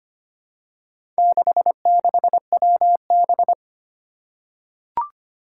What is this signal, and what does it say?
Morse code, heard as a keyed tone, about 25 wpm, 700 Hz. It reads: 66WB
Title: Four Challenging Letter-Number Combinations 25wpm